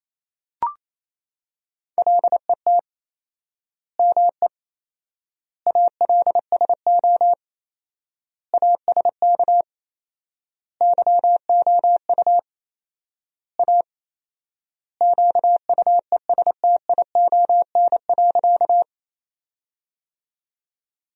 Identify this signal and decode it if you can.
Morse code, heard as a keyed tone, about 28 wpm, 700 Hz. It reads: LET ME ALSO ASK YOU A QUESTION.
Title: 4x Word-Spacing - Sentences from Top 400 Words - 28wpm